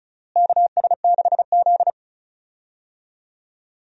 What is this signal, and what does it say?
Morse code, heard as a keyed tone, about 36 wpm, 700 Hz. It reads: KS6Z
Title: USA Call Signs - 4 Characters - Repeated 2x - 35wpm